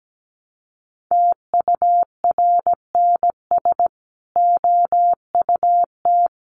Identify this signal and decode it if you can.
Morse code, heard as a keyed tone, about 17 wpm, 700 Hz. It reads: TURNS OUT
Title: Sets of 5 Words Encore 17wpm